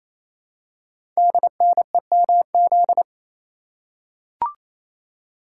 Morse code, heard as a keyed tone, about 27 wpm, 700 Hz. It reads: DNEMZ
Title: Code Groups - 5 Letters - 28wpm